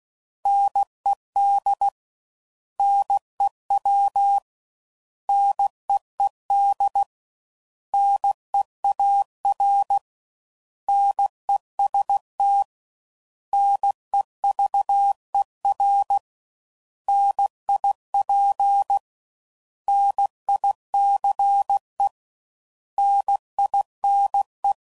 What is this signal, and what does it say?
Morse code, heard as a keyed tone, about 16 wpm, 800 Hz. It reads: NED NEW NEED NEAR NEST NEVER NIP NICE NINE
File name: Prefixes / By First Letter / Ns